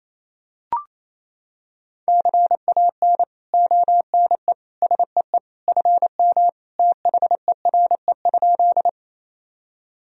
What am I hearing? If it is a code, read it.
Morse code, heard as a keyed tone, about 28 wpm, 700 Hz. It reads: CAN ONE SEE FM THERE?
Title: Sentences from Top 100 Words and Abbreviations 28wpm